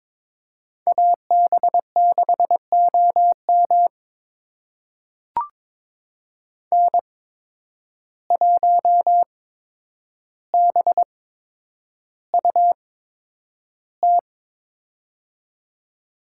Morse code, heard as a keyed tone, about 23 wpm, spaced as for 10 wpm, 700 Hz. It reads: AB6OM N1BUT
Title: USA Call Signs - 5 Characters - 8x Character Spacing - 22wpm